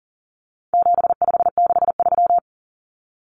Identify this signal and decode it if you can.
Morse code, heard as a keyed tone, about 40 wpm, 700 Hz. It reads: 7563
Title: Numbers - 4-Digit 40wpm